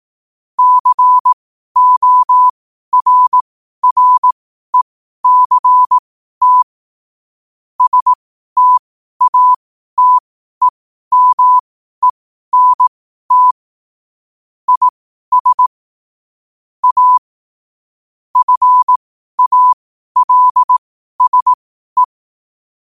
Morse code, heard as a keyed tone, about 18 wpm, spaced as for 12 wpm, 1000 Hz. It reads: CORRECT STATEMENT IS A FALSE
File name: qotdfiles / 12wpm